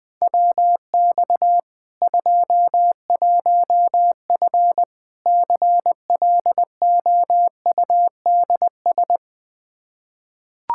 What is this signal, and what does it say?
Morse code, heard as a keyed tone, about 21 wpm, 700 Hz. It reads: WX 21F CLOUDS
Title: K1USN SST QSO Practice 20wpm